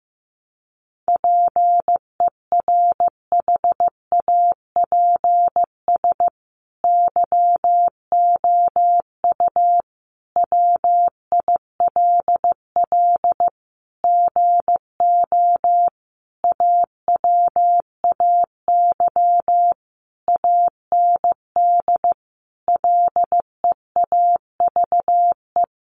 Morse code, heard as a keyed tone, about 15 wpm, 700 Hz. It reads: PERHAPS YOU WILL GO AWAY AND LEAVE
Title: Sentences from Top 1300 Words 15wpm